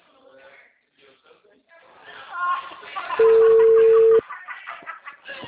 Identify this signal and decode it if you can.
Ringback tone (China, 450 Hz), with speech.